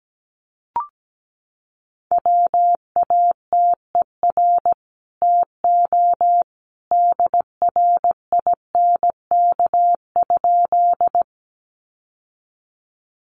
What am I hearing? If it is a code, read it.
Morse code, heard as a keyed tone, about 17 wpm, 700 Hz. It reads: WATER TO DRINK?
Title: Sentences from Top 1500 Words 17wpm